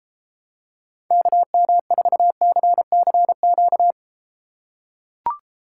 Morse code, heard as a keyed tone, about 33 wpm, 700 Hz. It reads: KM4CCQ